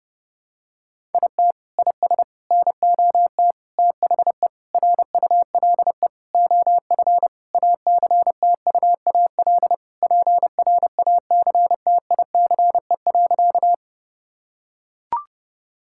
Morse code, heard as a keyed tone, about 30 wpm, 700 Hz. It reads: IT IS NOT THE RULE OF ACTUAL PRACTICE.
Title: Sentences from Top 1000 Words - Review Part 5 - 30wpm